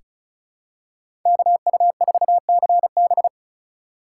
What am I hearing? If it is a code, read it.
Morse code, heard as a keyed tone, about 36 wpm, 700 Hz. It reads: KU4CB
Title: USA Call Signs - 5 Characters - 2x Character Spacing - 35wpm